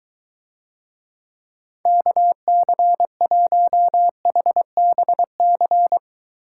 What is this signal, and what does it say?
Morse code, heard as a keyed tone, about 21 wpm, 700 Hz. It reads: KC1HBC